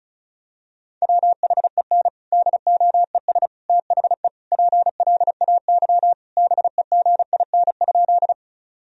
Morse code, heard as a keyed tone, about 34 wpm, 700 Hz. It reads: WHEN DOES THE PLAY BEGIN?
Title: Sentences from Top 1000 Words - Review Part 1 - 35wpm